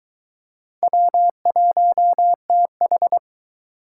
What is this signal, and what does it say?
Morse code, heard as a keyed tone, about 22 wpm, 700 Hz. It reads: W1TH